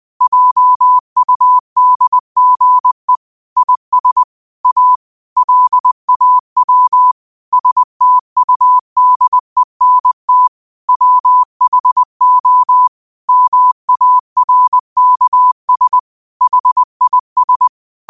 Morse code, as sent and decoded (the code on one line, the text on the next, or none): .--- ..- -.. --. / .. ... / .- / .-.. .- .-- / ... - ..- -.. -. - / .-- .... --- / -- .- .-. -.- ... / .... .. ...
JUDGE IS A LAW STUDENT WHO MARKS HIS